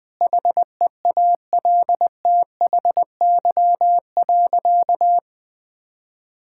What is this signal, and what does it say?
Morse code, heard as a keyed tone, about 20 wpm, 700 Hz. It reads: HEALTHY.
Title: Difficult Sentences with Hints - 20wpm